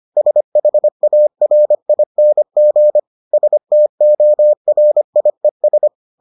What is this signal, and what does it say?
Morse code, heard as a keyed tone, about 25 wpm, 600 Hz. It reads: SHARING STORIES